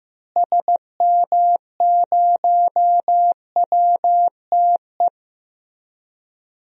Morse code, heard as a keyed tone, about 15 wpm, 700 Hz. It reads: SM0WTE
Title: DXCC-100 Call Signs - 6 Characters - Repeated 2x - 15wpm